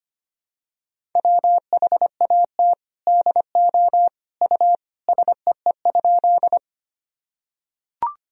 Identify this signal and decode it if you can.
Morse code, heard as a keyed tone, about 25 wpm, 700 Hz. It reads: WHAT DO U SEE?
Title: Sentences from Top 100 Words and Abbreviations 25wpm